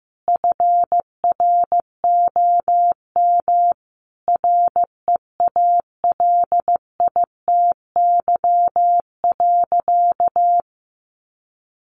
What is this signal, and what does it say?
Morse code, heard as a keyed tone, about 15 wpm, 700 Hz. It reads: FROM REALITY.
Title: Sentences from Top 1500 Words 15wpm